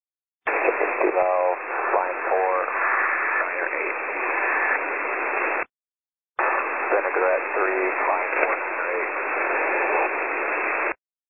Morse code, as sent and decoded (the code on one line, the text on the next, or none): none